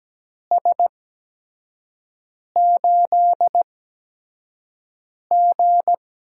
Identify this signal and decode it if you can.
Morse code, heard as a keyed tone, about 17 wpm, spaced as for 5 wpm, 700 Hz. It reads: S8G
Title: USA Call Signs - 4 Characters - 8x Character Spacing - 17wpm